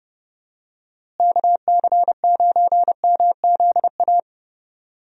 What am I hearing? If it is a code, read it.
Morse code, heard as a keyed tone, about 32 wpm, 700 Hz. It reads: KC9MZA
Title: USA Call Signs - 6 Characters - Repeated 3x - 30wpm